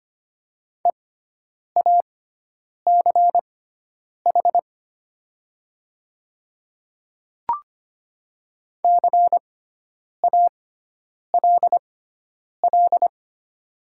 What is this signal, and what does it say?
Morse code, heard as a keyed tone, about 25 wpm, spaced as for 9 wpm, 700 Hz. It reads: EACH CALL
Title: 6x Character Spacing - Top 100 Words - 25wpm